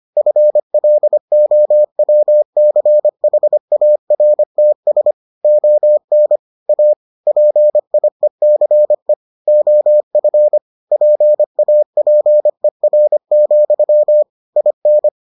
Morse code, heard as a keyed tone, about 25 wpm, 600 Hz. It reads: FLOWCHARTS ON A PIECE OF PAPER, IN